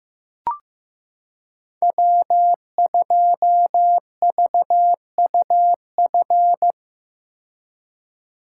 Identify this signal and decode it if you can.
Morse code, heard as a keyed tone, about 15 wpm, 700 Hz. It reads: W2VUF
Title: K1USN SST QSO Practice - 15wpm